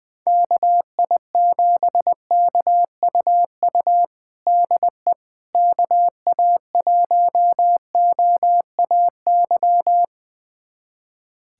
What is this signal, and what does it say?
Morse code, heard as a keyed tone, about 20 wpm, 700 Hz. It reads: KI7KUU DE KA1OAY
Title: USA Callsign DE Callsign 20wpm